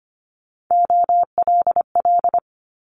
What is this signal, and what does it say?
Morse code, heard as a keyed tone, about 25 wpm, 700 Hz. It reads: OLL
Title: Multi-letter Phonemes Quick Fire 25wpm